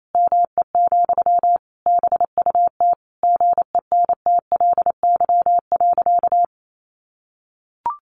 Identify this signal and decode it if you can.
Morse code, heard as a keyed tone, about 28 wpm, 700 Hz. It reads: ME, BUT GENTLY.
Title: Sentences from Top 1500 Words 28wpm